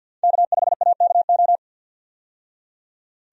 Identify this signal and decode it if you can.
Morse code, heard as a keyed tone, about 52 wpm, 700 Hz. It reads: K5AKO
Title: USA Call Signs - 5 Characters - 50wpm